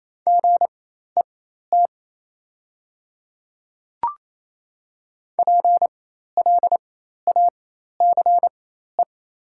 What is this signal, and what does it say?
Morse code, heard as a keyed tone, about 27 wpm, spaced as for 13 wpm, 690 Hz. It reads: GET PLACE
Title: 4x Character Spacing - Top 200 Words - 28wpm